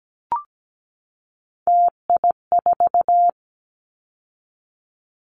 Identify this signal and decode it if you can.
Morse code, heard as a keyed tone, about 17 wpm, 700 Hz. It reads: TI4
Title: Callsign Prefixes for Top 100 DXCC - 17wpm